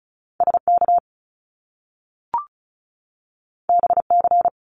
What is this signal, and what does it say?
Morse code, heard as a keyed tone, about 36 wpm, 700 Hz. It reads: SK BC
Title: Canadian Provinces and Territories 35wpm